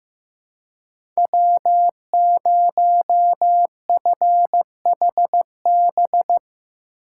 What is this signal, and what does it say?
Morse code, heard as a keyed tone, about 15 wpm, 700 Hz. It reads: W0FHB
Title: USA Call Signs - 5 Characters - 2x Character Spacing - 15wpm